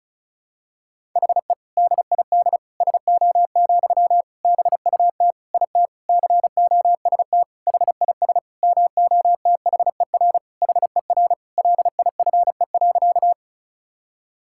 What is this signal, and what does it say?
Morse code, heard as a keyed tone, about 36 wpm, 700 Hz. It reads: HE DID SO, BUT IT COST HIS MOTHER HER LIFE.
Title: Sentences from Top 1000 Words - Review Part 5 - 35wpm